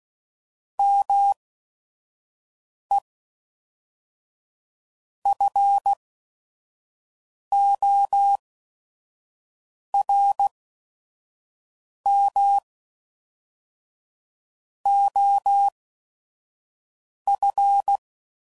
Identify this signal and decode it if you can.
Morse code, heard as a keyed tone, about 16 wpm, spaced as for 5 wpm, 800 Hz. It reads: ME FORM OF
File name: CW Code Groups @5WPM